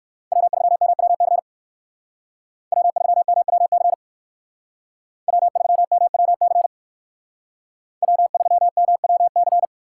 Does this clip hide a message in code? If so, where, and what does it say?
Yes: Morse code from 0.3 to 9.7 s, heard as a keyed tone, about 57 wpm, 700 Hz. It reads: W3MWC W3MWC W3MWC W3MWC